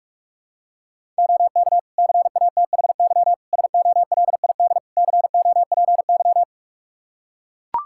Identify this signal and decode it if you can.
Morse code, heard as a keyed tone, about 41 wpm, 700 Hz. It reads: OK CATHY SOLID COPY
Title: K1USN SST QSO Practice 45wpm